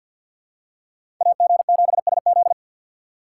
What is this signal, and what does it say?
Morse code, heard as a keyed tone, about 50 wpm, 700 Hz. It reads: A G 7 S Z
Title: USA Call Signs - 6x Character Spacing - 50wpm